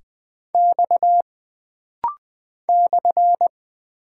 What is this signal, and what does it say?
Morse code, heard as a keyed tone, about 20 wpm, 700 Hz. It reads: X /
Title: Single Character (Letter:Number:Punctuation:Prosign) - Rapid-Fire - 20wpm